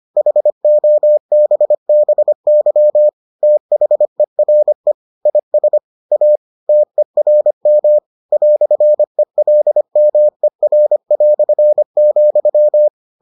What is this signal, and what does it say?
Morse code, heard as a keyed tone, about 25 wpm, 600 Hz. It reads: HOBBY THERE IS A TERM "ELMER",